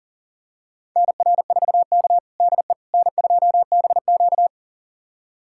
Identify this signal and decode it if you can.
Morse code, heard as a keyed tone, about 40 wpm, 700 Hz. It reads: NR4K DE N2BQ
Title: USA Callsign DE Callsign - 4 Characters - 40wpm